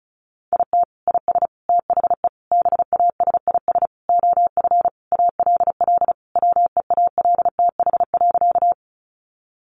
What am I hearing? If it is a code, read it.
Morse code, heard as a keyed tone, about 34 wpm, 700 Hz. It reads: IT IS THE BASIS OF ALL WEALTH.
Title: Sentences from Top 1200 Words 35wpm